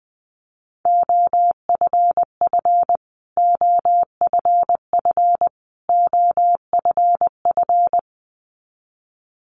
Morse code, heard as a keyed tone, about 20 wpm, 700 Hz. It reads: OFF OFF OFF
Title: Top 500 Words Repeated 20wpm